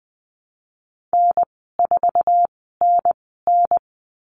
Morse code, heard as a keyed tone, about 20 wpm, spaced as for 14 wpm, 700 Hz. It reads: N4NN
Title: 055 Lesson 10g - Letter r - Call Signs 20wpm